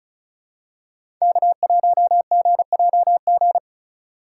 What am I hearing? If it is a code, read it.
Morse code, heard as a keyed tone, about 36 wpm, 700 Hz. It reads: K1GJG
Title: USA Call Signs - 5 Characters - 4x Character Spacing - 35wpm